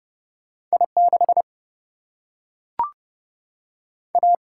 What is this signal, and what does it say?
Morse code, heard as a keyed tone, about 30 wpm, 700 Hz. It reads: I 6 A
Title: Single-Letter Single-Number 30wpm